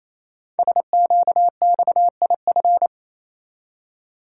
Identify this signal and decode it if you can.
Morse code, heard as a keyed tone, about 27 wpm, 700 Hz. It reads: SQXIF
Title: Code Groups - 5 Letter:Numbers - 28wpm